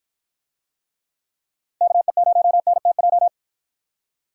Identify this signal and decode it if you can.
Morse code, heard as a keyed tone, about 53 wpm, 700 Hz. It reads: KE0NTJ